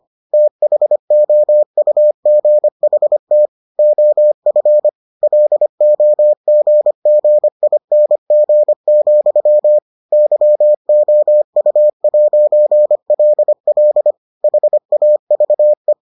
Morse code, heard as a keyed tone, about 25 wpm, 600 Hz. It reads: THOUGHT OF LOGGING, YOU'LL HAVE